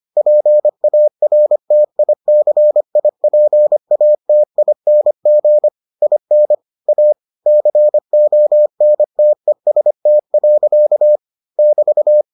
Morse code, heard as a keyed tone, about 25 wpm, 600 Hz. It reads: PARTICIPATING IN A CONTEST. <BT>